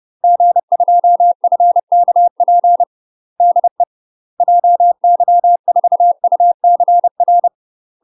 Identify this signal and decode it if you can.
Morse code, heard as a keyed tone, about 30 wpm, 700 Hz. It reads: G2FKP DE JY4UCR
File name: UK MASTER QSOs TRAINING/30WPM